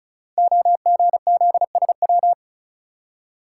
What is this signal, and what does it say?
Morse code, heard as a keyed tone, about 36 wpm, 690 Hz. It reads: OGZSW
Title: Code Groups - 5 Letter:Numbers - 35wpm